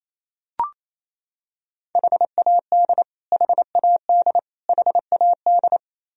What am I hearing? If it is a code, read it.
Morse code, heard as a keyed tone, about 28 wpm, 700 Hz. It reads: HAD HAD HAD
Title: Top 100 Words Repeated 28wpm